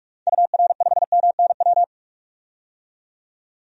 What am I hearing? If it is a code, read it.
Morse code, heard as a keyed tone, about 46 wpm, 700 Hz. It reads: UR5MNW
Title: DXCC-100 Call Signs - 6 Characters - 45wpm